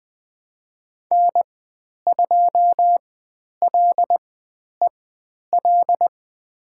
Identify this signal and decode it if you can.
Morse code, heard as a keyed tone, about 20 wpm, spaced as for 10 wpm, 700 Hz. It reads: N2LEL
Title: 095 Lesson 16g - Letter c - Call Signs 20wpm